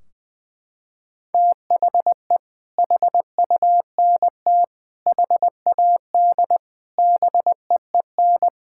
Morse code, heard as a keyed tone, about 20 wpm, 700 Hz. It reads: THE HUNT HAD BEEN
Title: Sentences from Top 1600 Words 20wpm